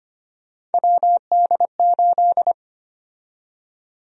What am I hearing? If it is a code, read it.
Morse code, heard as a keyed tone, about 25 wpm, 700 Hz. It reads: WD8
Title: USA Call Sign Prefixes - 25wpm